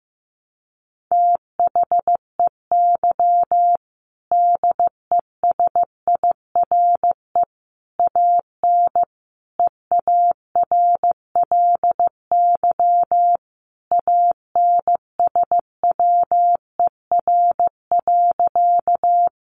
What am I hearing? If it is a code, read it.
Morse code, heard as a keyed tone, about 15 wpm, 700 Hz. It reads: THEY DESIRE AN EARLY ANSWER.
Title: Sentences from Top 1100 Words 15wpm